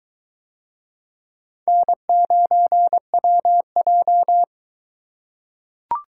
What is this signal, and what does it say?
Morse code, heard as a keyed tone, about 22 wpm, 700 Hz. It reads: N9WJ